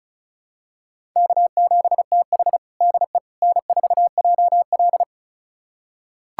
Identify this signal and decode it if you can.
Morse code, heard as a keyed tone, about 36 wpm, 700 Hz. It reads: K7TH DE N4JL
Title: USA Callsign DE Callsign - 4 Characters - 35wpm